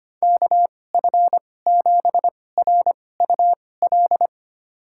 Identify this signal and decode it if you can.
Morse code, heard as a keyed tone, about 25 wpm, spaced as for 18 wpm, 700 Hz. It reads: KF7RUL
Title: USA Call Signs - 6 Characters - 2x Character Spacing - 25wpm